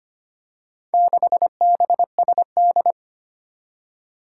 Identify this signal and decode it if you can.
Morse code, heard as a keyed tone, about 25 wpm, 700 Hz. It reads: 6BSD
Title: Four Challenging Letter-Number Combinations 25wpm